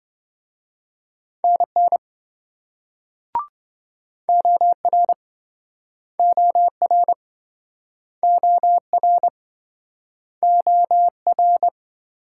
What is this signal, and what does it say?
Morse code, heard as a keyed tone, about 27 wpm, 700 Hz. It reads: NN OR OR OR OR